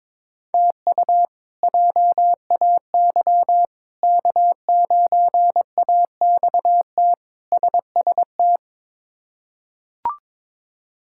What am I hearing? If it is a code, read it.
Morse code, heard as a keyed tone, about 22 wpm, 700 Hz. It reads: TU JAY K9AXT SST
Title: K1USN SST QSO Practice - 22wpm